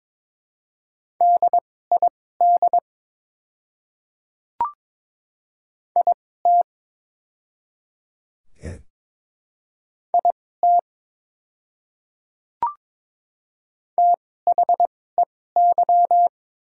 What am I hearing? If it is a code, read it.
Morse code, heard as a keyed tone, about 23 wpm, spaced as for 16 wpm, 690 Hz. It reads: DID IT IT THEY
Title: 2x Character Spacing - Top 100 Words - 22wpm